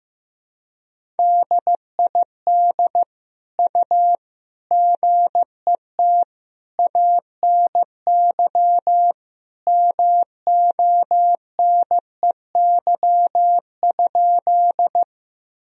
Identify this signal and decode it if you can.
Morse code, heard as a keyed tone, about 15 wpm, 700 Hz. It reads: DID U GET ANY MONEY?
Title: Sentences from Top 400 Words and Abbreviations 15wpm